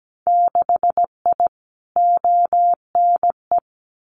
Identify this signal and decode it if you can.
Morse code, heard as a keyed tone, about 17 wpm, 700 Hz. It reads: 6I ONE
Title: Winter Field Day Exchanges Encore 17wpm